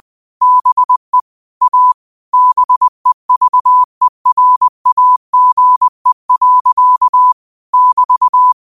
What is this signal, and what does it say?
Morse code, heard as a keyed tone, about 20 wpm, 1000 Hz. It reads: BE A BEVERAGE. <BT>
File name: qotdfiles / 20wpm